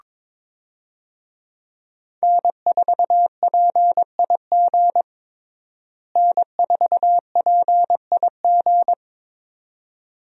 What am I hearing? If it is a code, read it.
Morse code, heard as a keyed tone, about 22 wpm, 700 Hz. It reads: N4PIG N4PIG
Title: USA Call Signs - 5 Characters - Repeated 2x - 22wpm